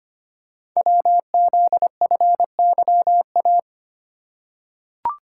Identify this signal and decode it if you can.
Morse code, heard as a keyed tone, about 25 wpm, 700 Hz. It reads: WZFYA
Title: Code Groups - 5 Letters - 25wpm